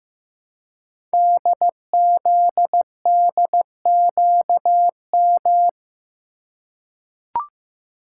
Morse code, heard as a keyed tone, about 15 wpm, 690 Hz. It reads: DZDQM
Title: Code Groups - 5 Letters - 15wpm